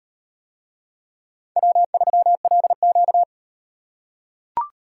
Morse code, heard as a keyed tone, about 36 wpm, 700 Hz. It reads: W3LQ